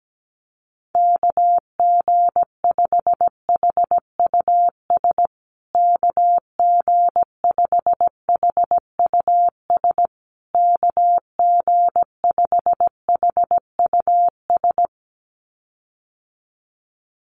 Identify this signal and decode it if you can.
Morse code, heard as a keyed tone, about 17 wpm, 700 Hz. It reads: KG5HUS KG5HUS KG5HUS
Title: USA Call Signs Repeated 3x 17wpm